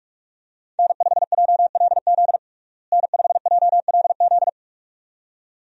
Morse code, heard as a keyed tone, about 45 wpm, 700 Hz. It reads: N5JLZ N5JLZ
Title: USA Call Signs - 5 Characters - Repeated 2x - 45wpm